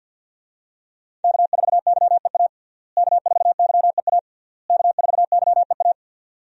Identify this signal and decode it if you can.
Morse code, heard as a keyed tone, about 48 wpm, 700 Hz. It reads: K4YEA K4YEA K4YEA
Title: USA Call Signs - 5 Characters - Repeated 3x - 50wpm